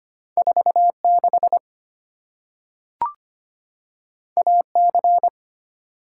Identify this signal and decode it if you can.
Morse code, heard as a keyed tone, about 25 wpm, 700 Hz. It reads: 46 AC
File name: Two Letter-Number Combinations - 25wpm